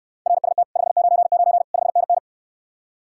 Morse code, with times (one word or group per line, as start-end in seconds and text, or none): 0.3-0.6 RST
0.8-1.6 599
1.7-2.2 5NN